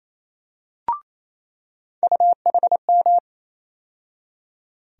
Morse code, heard as a keyed tone, about 28 wpm, 700 Hz. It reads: UHM
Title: USA Call Sign Postfixes - 3 Letters 28wpm